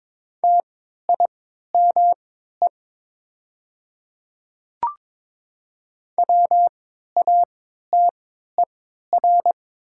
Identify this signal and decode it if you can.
Morse code, heard as a keyed tone, about 22 wpm, spaced as for 12 wpm, 700 Hz. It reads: TIME WATER
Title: 3x Character Spacing - Top 100 Words - 22wpm